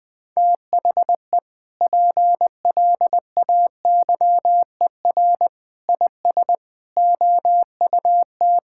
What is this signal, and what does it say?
Morse code, heard as a keyed tone, about 20 wpm, 700 Hz. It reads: THE PLAYER IS OUT
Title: Sentences with 6-Word Phrases - 20wpm